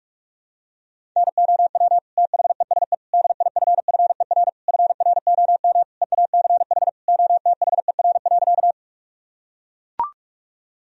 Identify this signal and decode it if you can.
Morse code, heard as a keyed tone, about 41 wpm, 700 Hz. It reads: NOW THESE DIFFER FROM EACH OTHER.
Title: Sentences from Top 1000 Words - Review Part 1 - 45wpm